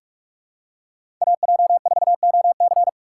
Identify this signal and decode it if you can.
Morse code, heard as a keyed tone, about 41 wpm, 700 Hz. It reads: AJ4OC
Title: USA Call Signs - 5 Characters - 2x Character Spacing - 45wpm